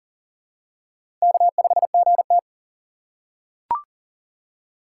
Morse code, heard as a keyed tone, about 42 wpm, 700 Hz. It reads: K5GT
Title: USA Call Signs - 4 Characters - 4x Character Spacing - 40wpm